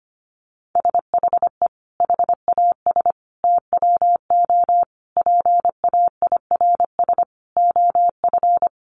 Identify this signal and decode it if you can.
Morse code, heard as a keyed tone, about 25 wpm, 700 Hz. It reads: SHE HAS TWO PAIRS OF
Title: Additional Easy Phrases 25wpm